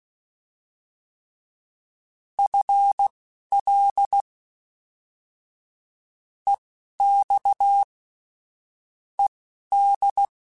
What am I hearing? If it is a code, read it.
Morse code, heard as a keyed tone, about 16 wpm, spaced as for 12 wpm, 800 Hz. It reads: FL EX ED
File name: FL / FL mp3s / Fast FL @10